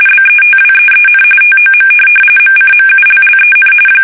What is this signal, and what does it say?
RTTY, about 45.45 baud. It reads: BETWEEN TOM MCMULLEN
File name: fsk850 / teletype-RTTY